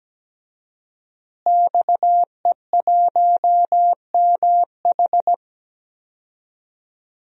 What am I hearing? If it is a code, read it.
Morse code, heard as a keyed tone, about 17 wpm, 700 Hz. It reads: XE1MH
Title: DXCC-100 Call Signs - 5 Characters - Repeated 3x - 17wpm